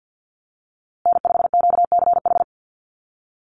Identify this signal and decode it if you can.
Morse code, heard as a keyed tone, about 50 wpm, 700 Hz. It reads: N5QCH
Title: USA Call Signs Repeated 3x 50wpm